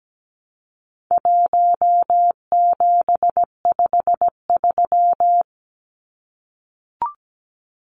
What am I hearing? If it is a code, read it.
Morse code, heard as a keyed tone, about 17 wpm, 700 Hz. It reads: 1753
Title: Numbers - 4-Digit 17wpm